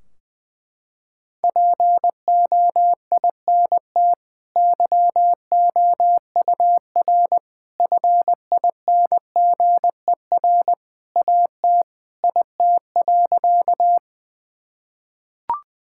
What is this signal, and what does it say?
Morse code, heard as a keyed tone, about 20 wpm, 700 Hz. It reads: POINT YOUR FINGER AT IT.
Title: Sentences from Top 1600 Words 20wpm